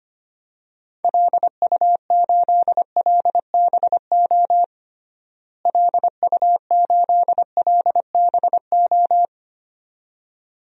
Morse code, heard as a keyed tone, about 25 wpm, 700 Hz. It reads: LU8LBO LU8LBO
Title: DXCC-100 Call Signs - 6 Characters - Repeated 2x - 25wpm